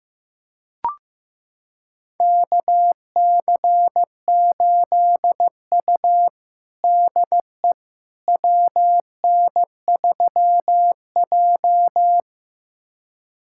Morse code, heard as a keyed tone, about 15 wpm, 700 Hz. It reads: KC8U DE WN3J
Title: USA Callsign DE Callsign - 4 Characters - 15wpm